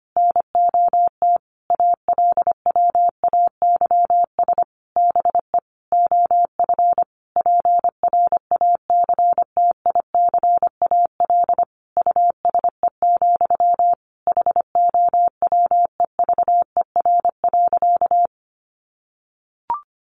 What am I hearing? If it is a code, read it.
Morse code, heard as a keyed tone, about 25 wpm, 700 Hz. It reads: NOT ALWAYS BE OF PRACTICAL USE, HOWEVER.
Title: Sentences from Top 1400 Words 25wpm